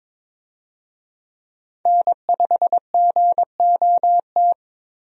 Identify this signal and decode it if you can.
Morse code, heard as a keyed tone, about 22 wpm, 700 Hz. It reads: N5GOT
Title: USA Call Signs - 5 Characters - Repeated 2x - 22wpm